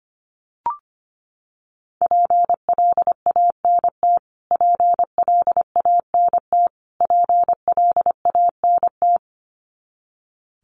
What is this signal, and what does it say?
Morse code, heard as a keyed tone, about 24 wpm, 700 Hz. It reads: PLANT PLANT PLANT
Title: Top 300 Words Repeated 25wpm